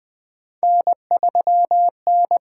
Morse code, heard as a keyed tone, about 20 wpm, 700 Hz. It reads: N3N
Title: USA Call Signs - 5 Characters - 8x Character Spacing - 20wpm